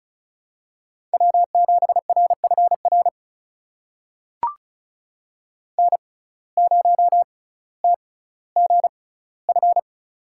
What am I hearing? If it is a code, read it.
Morse code, heard as a keyed tone, about 36 wpm, spaced as for 18 wpm, 700 Hz. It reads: W7RFR N0TGF
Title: USA Call Signs - 5 Characters - 6x Character Spacing - 35wpm